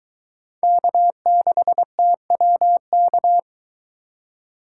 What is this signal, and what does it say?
Morse code, heard as a keyed tone, about 22 wpm, 700 Hz. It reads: K6TWK